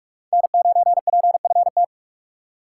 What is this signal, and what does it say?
Morse code, heard as a keyed tone, about 44 wpm, 690 Hz. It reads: N9PFT
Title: USA Call Signs - 5 Characters - 6x Character Spacing - 45wpm